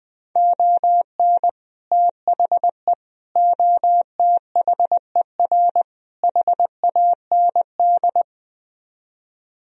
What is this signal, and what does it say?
Morse code, heard as a keyed tone, about 20 wpm, 700 Hz. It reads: ON THE OTHER HAND